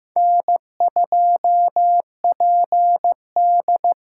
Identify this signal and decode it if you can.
Morse code, heard as a keyed tone, about 15 wpm, 700 Hz. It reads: N2PD
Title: USA Call Signs - 5 Characters - 8x Character Spacing - 15wpm